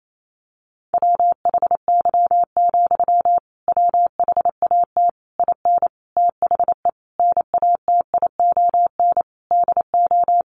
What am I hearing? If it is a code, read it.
Morse code, heard as a keyed tone, about 28 wpm, 700 Hz. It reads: WHY, WHAT IN THE NATION DO
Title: Sentences from Top 900 Words 28wpm